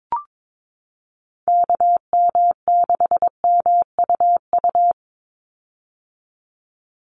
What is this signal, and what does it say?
Morse code, heard as a keyed tone, about 22 wpm, 700 Hz. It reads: KM6MUU
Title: USA Call Signs 22wpm